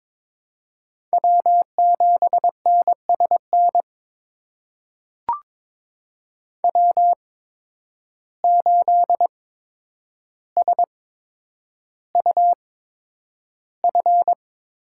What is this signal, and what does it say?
Morse code, heard as a keyed tone, about 22 wpm, spaced as for 9 wpm, 700 Hz. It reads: W7NSN W8SUF
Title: USA Call Signs - 5 Characters - 8x Character Spacing - 22wpm